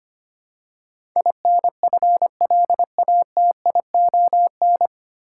Morse code, heard as a keyed tone, about 25 wpm, 700 Hz. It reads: INFLATION
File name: Nine Letter Words - 25wpm